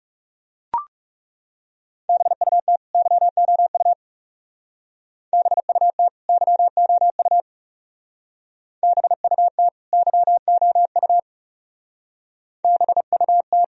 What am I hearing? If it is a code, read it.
Morse code, heard as a keyed tone, about 35 wpm, 700 Hz. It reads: BUT YOU BUT YOU BUT YOU BUT
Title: Speed-Racing - Sets of 2 Words 30wpm